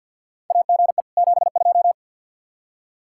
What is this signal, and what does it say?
Morse code, heard as a keyed tone, about 46 wpm, 700 Hz. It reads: AGE 72
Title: Practice QSO - Exchange 3 - More Info - 50wpm